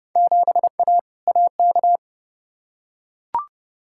Morse code, heard as a keyed tone, about 30 wpm, 700 Hz. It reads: 7A AK
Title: Field Day Exchanges Encore - 30wpm